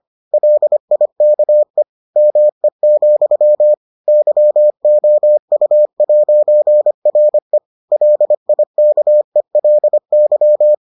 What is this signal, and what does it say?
Morse code, heard as a keyed tone, about 25 wpm, 600 Hz. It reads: LIKE ME, YOU'RE LIKELY